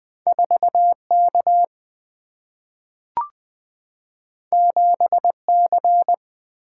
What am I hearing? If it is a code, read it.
Morse code, heard as a keyed tone, about 20 wpm, 700 Hz. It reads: 4 K 7 C
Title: Single-Letter Single-Number 20wpm